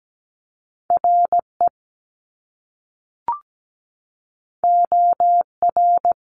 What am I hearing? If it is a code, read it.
Morse code, heard as a keyed tone, about 17 wpm, 700 Hz. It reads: RE OR
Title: Top Two Letter Combinations in Words 17wpm